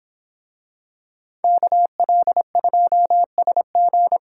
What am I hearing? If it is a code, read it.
Morse code, heard as a keyed tone, about 23 wpm, 700 Hz. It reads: KL2SG